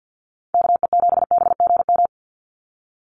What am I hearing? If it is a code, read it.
Morse code, heard as a keyed tone, about 50 wpm, 700 Hz. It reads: KE7BGM
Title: NA SOTA Activator Callsigns 50wpm